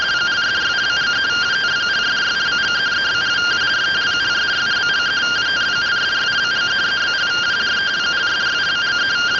Two-tone teleprinter signal, about 170 Hz shift.